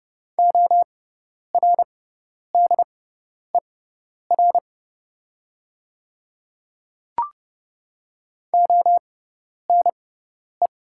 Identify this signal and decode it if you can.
Morse code, heard as a keyed tone, about 31 wpm, spaced as for 10 wpm, 700 Hz. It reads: ORDER ONE
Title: 6x Character Spacing - Top 500 Words - 30wpm